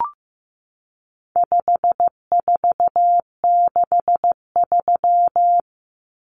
Morse code, heard as a keyed tone, about 15 wpm, 700 Hz. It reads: 5463
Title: Numbers - 4-Digit 15wpm